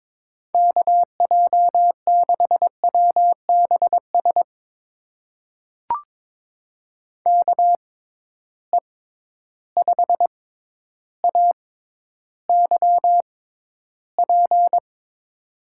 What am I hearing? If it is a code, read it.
Morse code, heard as a keyed tone, about 22 wpm, spaced as for 11 wpm, 700 Hz. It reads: KJ6WBS KE5AYP
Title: USA Call Signs - 6 Characters - 6x Character Spacing - 22wpm